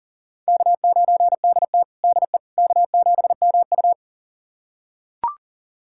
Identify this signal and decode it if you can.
Morse code, heard as a keyed tone, about 40 wpm, 700 Hz. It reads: K9DT DE K7MU
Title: USA Callsign DE Callsign - 4 Characters - 40wpm